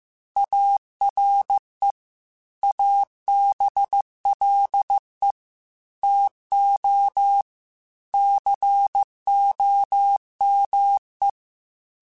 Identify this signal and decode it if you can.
Morse code, heard as a keyed tone, about 15 wpm, 800 Hz. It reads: ARE ABLE TO COME
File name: QSOs / QSOs-mp3s / WA7QOX 1@13